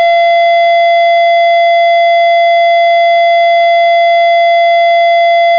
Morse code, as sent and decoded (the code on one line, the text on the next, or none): none